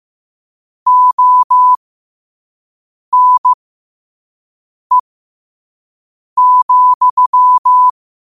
Morse code, heard as a keyed tone, about 15 wpm, spaced as for 5 wpm, 1000 Hz. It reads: ONE,
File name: qotdfiles / 5wpm